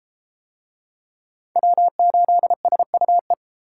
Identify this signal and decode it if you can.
Morse code, heard as a keyed tone, about 32 wpm, 700 Hz. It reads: W8SUE